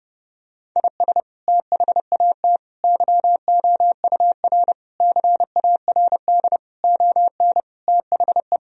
Morse code, heard as a keyed tone, about 30 wpm, 700 Hz. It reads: IS THAT YOUR CARD ON THE
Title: Sentences from Top 1000 Words - Review Part 4 - 30wpm